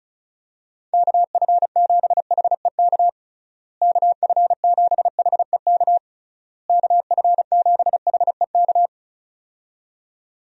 Morse code, heard as a keyed tone, about 34 wpm, 700 Hz. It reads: KF7HEK KF7HEK KF7HEK
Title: USA Call Signs - 6 Characters - Repeated 3x - 35wpm